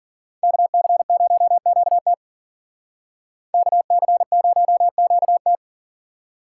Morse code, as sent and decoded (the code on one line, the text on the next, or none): -.- -.-. ----- --.- - / -.- -.-. ----- --.- -
KC0QT KC0QT